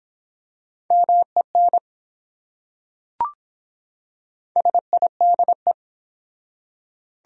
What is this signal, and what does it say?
Morse code, heard as a keyed tone, about 24 wpm, 700 Hz. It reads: MEN SIDE